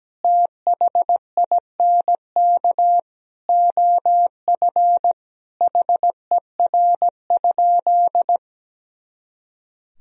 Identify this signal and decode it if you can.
Morse code, heard as a keyed tone, about 17 wpm, 700 Hz. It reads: THINK OF HER?
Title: Sentences from Top 200 Words and Abbreviations 17wpm